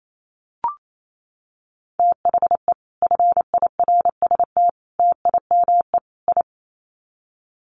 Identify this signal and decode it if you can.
Morse code, heard as a keyed tone, about 28 wpm, 700 Hz. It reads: THE FIRST TIME I
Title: Sets of 4 Words 28wpm